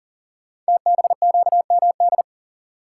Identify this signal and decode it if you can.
Morse code, heard as a keyed tone, about 38 wpm, 700 Hz. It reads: TBQMD
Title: Code Groups - 5 Letters - 40wpm